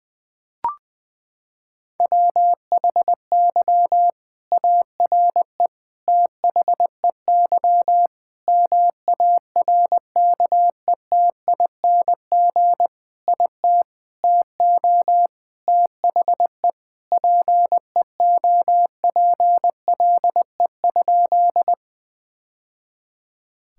Morse code, heard as a keyed tone, about 20 wpm, 700 Hz. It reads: WHY ARE THEY MARKETING IT TO THE PEOPLE?
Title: Sentences from Top 1000 Words - Review Part 5 - 20wpm